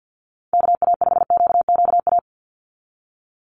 Morse code, heard as a keyed tone, about 50 wpm, 700 Hz. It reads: KA5QQA
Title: NA SOTA Chaser Callsigns 50wpm